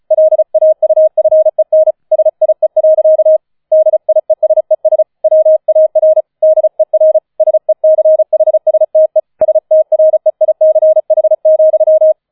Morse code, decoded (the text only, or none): LAUFEN SIE. DIESES WAR DER SECHSTE STREICH,